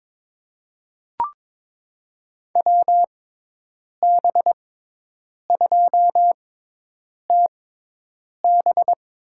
Morse code, heard as a keyed tone, about 22 wpm, spaced as for 8 wpm, 700 Hz. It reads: WB2TB
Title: USA Call Signs - 6 Characters - 6x Character Spacing - 22wpm